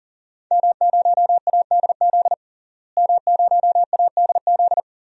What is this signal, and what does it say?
Morse code, heard as a keyed tone, about 40 wpm, 700 Hz. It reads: M0ADZ M0ADZ
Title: DXCC-100 Call Signs - 5 Characters - Repeated 3x - 40wpm